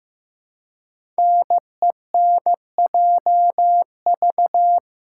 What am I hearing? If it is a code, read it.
Morse code, heard as a keyed tone, about 15 wpm, 700 Hz. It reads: NENJV
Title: Code Groups - 5 Letters - 15wpm